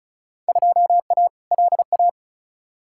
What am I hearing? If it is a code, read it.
Morse code, heard as a keyed tone, about 36 wpm, 700 Hz. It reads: 2A LA
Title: Field Day Exchanges - 35wpm